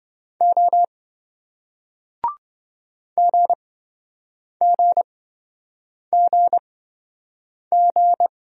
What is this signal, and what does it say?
Morse code, heard as a keyed tone, about 27 wpm, spaced as for 7 wpm, 700 Hz. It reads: O GGGG